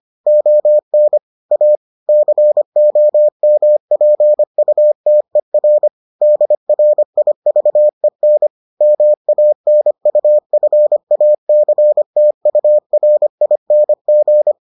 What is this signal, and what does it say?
Morse code, heard as a keyed tone, about 25 wpm, 600 Hz. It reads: ON A COMPUTER DRIVEN MANUFACTURING